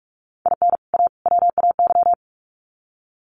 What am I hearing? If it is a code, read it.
Morse code, heard as a keyed tone, about 45 wpm, 700 Hz. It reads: IN A WAY
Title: Sets of 3 Words 45wpm